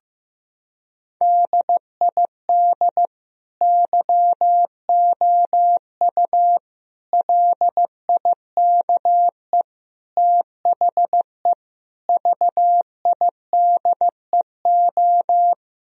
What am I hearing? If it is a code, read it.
Morse code, heard as a keyed tone, about 16 wpm, 700 Hz. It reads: DID YOU LIKE THE VIDEO
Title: Sentences from Top 1000 Words - Review Part 4 - 15wpm